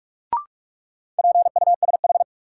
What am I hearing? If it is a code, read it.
Morse code, heard as a keyed tone, about 44 wpm, 700 Hz. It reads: PUSH
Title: Sentences from Top 1300 Words - New Word Drill Alt - 45wpm